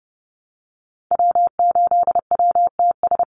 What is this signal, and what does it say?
Morse code, heard as a keyed tone, about 30 wpm, 700 Hz. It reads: W8WTS
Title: Call Signs Repeated 2x - 30wpm